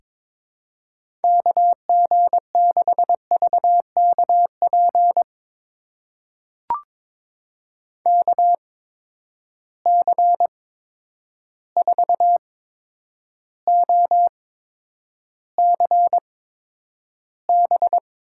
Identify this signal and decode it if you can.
Morse code, heard as a keyed tone, about 23 wpm, spaced as for 10 wpm, 700 Hz. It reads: KG6VKP KC4OCB
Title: USA Call Signs - 6 Characters - 8x Character Spacing - 22wpm